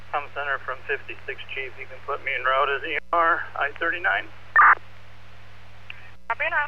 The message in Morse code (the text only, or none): none